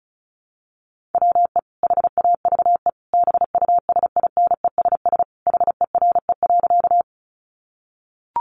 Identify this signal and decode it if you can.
Morse code, heard as a keyed tone, about 34 wpm, 700 Hz. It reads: WE HAVE BUSINESS HERE.
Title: Sentences from Top 600 Words 35wpm